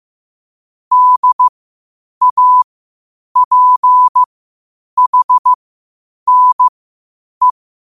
Morse code, heard as a keyed tone, about 15 wpm, spaced as for 8 wpm, 1000 Hz. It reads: DAPHNE
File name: qotdfiles / 8wpm